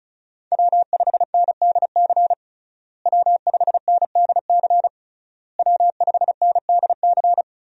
Morse code, heard as a keyed tone, about 36 wpm, 700 Hz. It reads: W5NDC W5NDC W5NDC
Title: USA Call Signs - 5 Characters - Repeated 3x - 35wpm